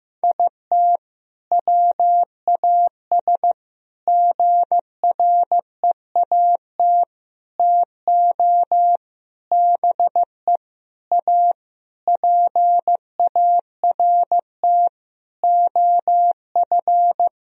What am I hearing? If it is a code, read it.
Morse code, heard as a keyed tone, about 15 wpm, 700 Hz. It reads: IT WAS GREAT TO BE A PART OF
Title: Sentences from Top 1000 Words - Review Part 2 - 15wpm